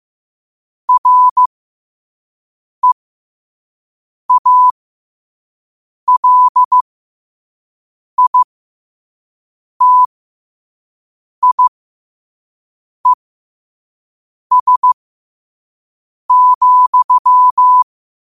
Morse code, heard as a keyed tone, about 15 wpm, spaced as for 5 wpm, 1000 Hz. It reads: REALITIES,
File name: qotdfiles / 5wpm